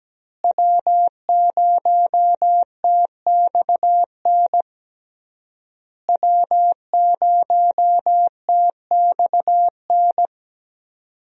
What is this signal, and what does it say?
Morse code, heard as a keyed tone, about 17 wpm, 700 Hz. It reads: W0TXN W0TXN
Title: USA Call Signs - 5 Characters - Repeated 3x - 17wpm